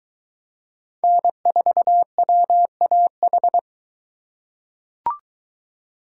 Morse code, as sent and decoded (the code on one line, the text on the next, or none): -. ....- .-- .- ....
N4WAH